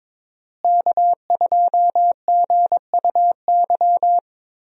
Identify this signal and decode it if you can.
Morse code, heard as a keyed tone, about 22 wpm, 700 Hz. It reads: K2GUY
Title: USA Call Signs - 5 Characters - Repeated 3x - 22wpm